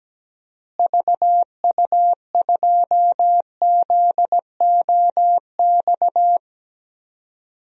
Morse code, heard as a keyed tone, about 17 wpm, 700 Hz. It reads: VU2ZOX
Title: DXCC-100 Call Signs - 6 Characters - Repeated 2x - 17wpm